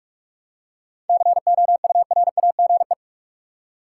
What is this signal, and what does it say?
Morse code, heard as a keyed tone, about 47 wpm, 700 Hz. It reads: COURAGE
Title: Seven Letter Words - 45wpm